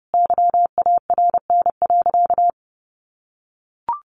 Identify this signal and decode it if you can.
Morse code, heard as a keyed tone, about 30 wpm, 700 Hz. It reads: YARN.
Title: Additional Easy Phrases 30wpm